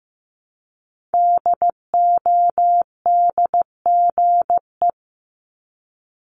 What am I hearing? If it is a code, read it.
Morse code, heard as a keyed tone, about 15 wpm, 700 Hz. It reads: DODGE
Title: Multi-letter Phonemes Repeated 3x 15wpm